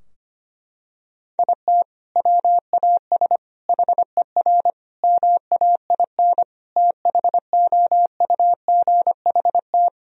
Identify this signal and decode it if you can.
Morse code, heard as a keyed tone, about 25 wpm, 700 Hz. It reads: IT WAS HER MAIN THOUGHT
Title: Sentences from Top 1000 Words - Review Part 2 - 25wpm